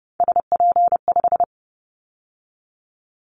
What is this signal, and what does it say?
Morse code, heard as a keyed tone, about 30 wpm, 700 Hz. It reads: SP5
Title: Callsign Prefixes for Top 25 DXCC - 30wpm